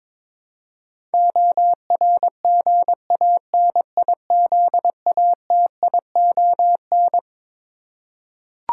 Morse code, heard as a keyed tone, about 22 wpm, 700 Hz. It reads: ORGANIZATION
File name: Twelve Letter Words - 22wpm